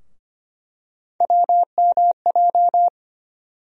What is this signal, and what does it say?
Morse code, heard as a keyed tone, about 25 wpm, 700 Hz. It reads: WMJ
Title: USA Call Sign Postfixes - 3 Letters 25wpm